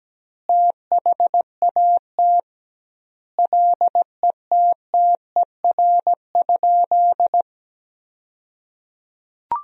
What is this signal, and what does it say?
Morse code, heard as a keyed tone, about 17 wpm, 700 Hz. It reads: THAT LETTER?
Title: 2x Word-Spacing - Sentences from Top 300 Words - 17wpm